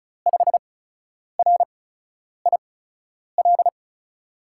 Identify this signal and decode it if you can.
Morse code, heard as a keyed tone, about 36 wpm, spaced as for 10 wpm, 700 Hz. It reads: 5RIL
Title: USA Call Signs - 6 Characters - 8x Character Spacing - 35wpm